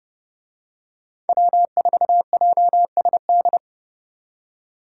Morse code, heard as a keyed tone, about 29 wpm, 700 Hz. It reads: W4JSD